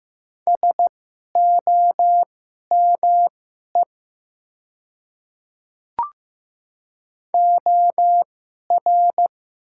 Morse code, heard as a keyed tone, about 15 wpm, spaced as for 11 wpm, 700 Hz. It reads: SOME OR
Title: 2x Character Spacing - Top 100 Words - 15wpm